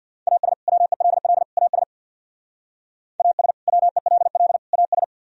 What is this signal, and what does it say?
Morse code, heard as a keyed tone, about 60 wpm, 700 Hz. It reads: AS WELL AS AS WELL AS
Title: Speed-Racing - Sets of 3 Words 50wpm